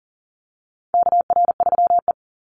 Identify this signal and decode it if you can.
Morse code, heard as a keyed tone, about 38 wpm, 700 Hz. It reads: KR3E
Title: Call Signs - 40wpm